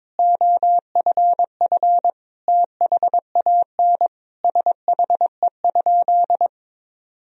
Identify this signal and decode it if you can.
Morse code, heard as a keyed tone, about 22 wpm, 700 Hz. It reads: OFF THAN SHE?
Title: Sentences from Top 1000 Words - Review Part 3 - 22wpm